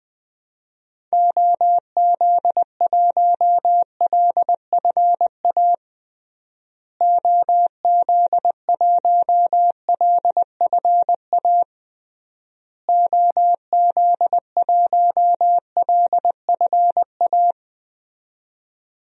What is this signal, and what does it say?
Morse code, heard as a keyed tone, about 21 wpm, 700 Hz. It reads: OZ1LFA OZ1LFA OZ1LFA
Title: DXCC-100 Call Signs - 6 Characters - Repeated 3x - 20wpm